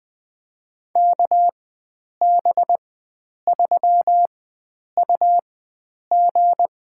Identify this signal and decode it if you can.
Morse code, heard as a keyed tone, about 20 wpm, spaced as for 9 wpm, 700 Hz. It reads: KB3UG
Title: USA Call Signs - 4x Character Spacing - 20wpm